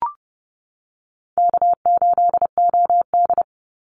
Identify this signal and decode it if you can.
Morse code, heard as a keyed tone, about 30 wpm, 700 Hz. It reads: K8OD
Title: Easy Call Signs - 30wpm